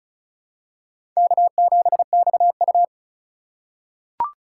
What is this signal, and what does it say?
Morse code, heard as a keyed tone, about 36 wpm, 700 Hz. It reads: K7XU
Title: USA Call Signs - 4 Characters - 2x Character Spacing - 35wpm